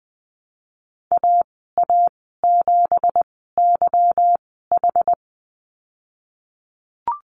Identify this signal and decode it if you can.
Morse code, heard as a keyed tone, about 20 wpm, spaced as for 15 wpm, 700 Hz. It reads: AA7YH
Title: 231 Lesson 37g - Letter x - Call Signs 20wpm